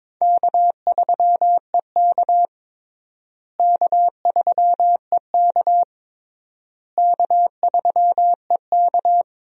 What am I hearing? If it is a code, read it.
Morse code, heard as a keyed tone, about 22 wpm, 700 Hz. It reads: K3EK K3EK K3EK
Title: USA Call Signs - 4 Characters - Repeated 3x - 22wpm